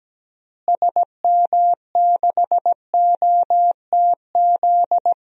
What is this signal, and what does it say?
Morse code, heard as a keyed tone, about 17 wpm, 700 Hz. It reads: SM6OTZ
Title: DXCC-100 Call Signs - 6 Characters - Repeated 2x - 17wpm